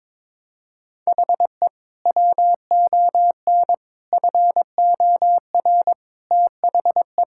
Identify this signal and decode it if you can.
Morse code, heard as a keyed tone, about 21 wpm, 700 Hz. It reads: HE WON FOR THE
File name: Sentences with 7-Word Phrases - 22wpm